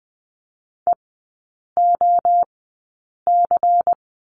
Morse code, heard as a keyed tone, about 19 wpm, spaced as for 8 wpm, 700 Hz. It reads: EOC
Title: 191 Lesson 31e - Letter v - Three Characters 20wpm